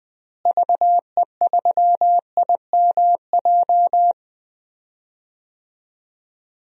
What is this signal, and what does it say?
Morse code, heard as a keyed tone, about 20 wpm, 700 Hz. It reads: VE3IMJ
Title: DXCC-100 Call Signs - 6 Characters - Repeated 2x - 20wpm